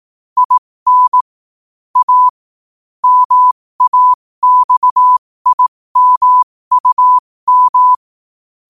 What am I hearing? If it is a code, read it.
Morse code, heard as a keyed tone, about 18 wpm, spaced as for 15 wpm, 1000 Hz. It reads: IN A MAXIMUM
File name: qotdfiles / 15wpm